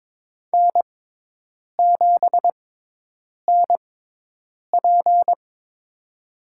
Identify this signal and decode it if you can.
Morse code, heard as a keyed tone, about 22 wpm, spaced as for 8 wpm, 700 Hz. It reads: N7NP
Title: USA Call Signs - 6x Character Spacing - 22wpm